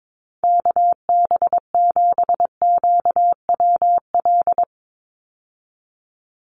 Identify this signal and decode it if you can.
Morse code, heard as a keyed tone, about 22 wpm, 700 Hz. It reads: KB7QWL
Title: USA Call Signs 22wpm